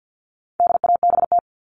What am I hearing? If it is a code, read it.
Morse code, heard as a keyed tone, about 50 wpm, 700 Hz. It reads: DABT
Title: All Cut Numbers - 4-Digit 50wpm